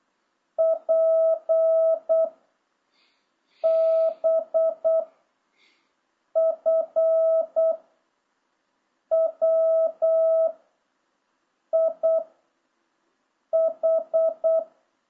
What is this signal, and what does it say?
Morse code, heard as a keyed tone, about 8 wpm, 660 Hz. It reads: P B F W I H